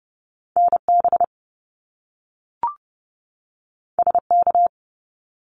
Morse code, heard as a keyed tone, about 30 wpm, 700 Hz. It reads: NB SK
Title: Canadian Provinces and Territories 30wpm